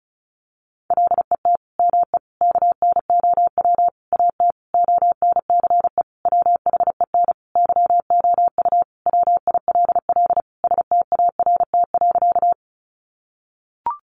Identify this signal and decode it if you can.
Morse code, heard as a keyed tone, about 34 wpm, 700 Hz. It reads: LET ME KNOW AT ONCE WHEN YOU WILL START.
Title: Sentences from Top 300 Words 35wpm